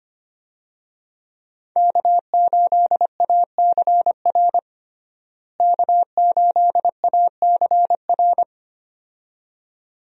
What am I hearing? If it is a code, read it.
Morse code, heard as a keyed tone, about 25 wpm, 700 Hz. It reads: K8ACR K8ACR
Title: USA Call Signs - 5 Characters - Repeated 2x - 25wpm